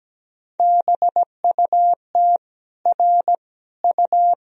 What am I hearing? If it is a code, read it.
Morse code, heard as a keyed tone, about 17 wpm, 700 Hz. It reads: BUT R U
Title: Sentences from Top 300 Words and Abbreviations 17wpm